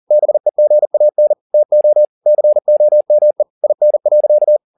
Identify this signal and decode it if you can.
Morse code, heard as a keyed tone, about 40 wpm, 600 Hz. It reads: BEGAN TO COME IN.
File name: the old man and the sea 2 40wpm